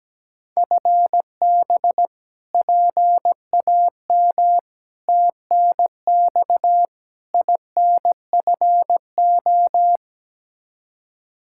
Morse code, heard as a keyed tone, about 17 wpm, 700 Hz. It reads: FB PAM TNX INFO
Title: K1USN SST QSO Practice 17wpm